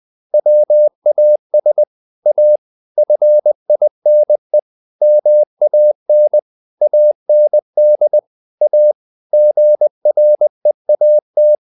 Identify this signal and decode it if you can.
Morse code, heard as a keyed tone, about 20 wpm, 600 Hz. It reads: WAS A FINE MAN AND A GREAT
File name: the old man and the sea part 2 20wpm